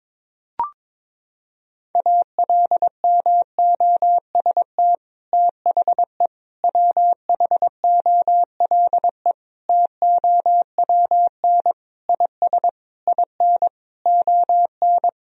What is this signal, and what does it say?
Morse code, heard as a keyed tone, about 22 wpm, 700 Hz. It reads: ALMOST THE WHOLE TOWN IS IN ON
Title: Sentences from Top 1000 Words - Review Part 4 - 22wpm